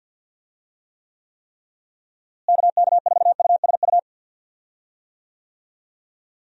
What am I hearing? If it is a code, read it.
Morse code, heard as a keyed tone, about 47 wpm, 700 Hz. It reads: KK4USU
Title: USA Call Signs - 6 Characters - Repeated 3x - 50wpm